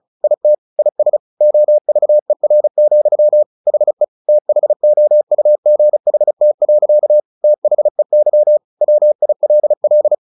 Morse code, heard as a keyed tone, about 34 wpm, 600 Hz. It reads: IT IS OVER, HE THOUGHT. THEY WILL
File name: the old man and the sea 2 35wpm